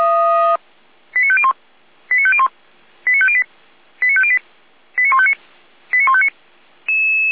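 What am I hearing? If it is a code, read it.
Five-tone selective call (ZVEI), heard as quick runs of five tones, tones of about 70 ms. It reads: 89501 89501 89508 89508 89159 89159
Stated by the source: ZVEI five-tone call received on an SDR by feuerwehr-orlamuende.org (fire brigade Orlamünde).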